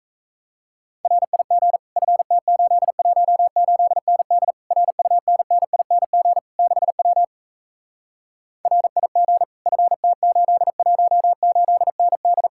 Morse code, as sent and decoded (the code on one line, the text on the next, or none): .-. .. --. / ..-. - ---.. .---- ---.. -. -.. / .-. ..- -. -. .. -. --. / -.... .-- / .-. .. --. / ..-. - ---.. .---- ---.. -. -..
RIG FT818ND RUNNING 6W RIG FT818ND